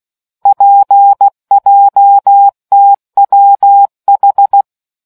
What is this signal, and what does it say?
Morse code, heard as a keyed tone, about 16 wpm, 800 Hz. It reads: PJTWH